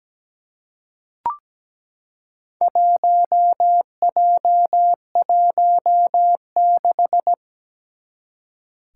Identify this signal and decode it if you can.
Morse code, heard as a keyed tone, about 17 wpm, 700 Hz. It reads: 1J16
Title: Four Challenging Letter-Number Combinations 17wpm